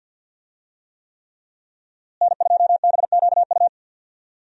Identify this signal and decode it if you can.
Morse code, heard as a keyed tone, about 47 wpm, 700 Hz. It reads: N2BQU
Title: USA Call Signs - 5 Characters - Repeated 3x - 50wpm